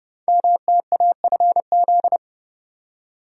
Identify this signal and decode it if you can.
Morse code, heard as a keyed tone, about 30 wpm, 700 Hz. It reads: MTAFZ
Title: Code Groups - 5 Letters - 30wpm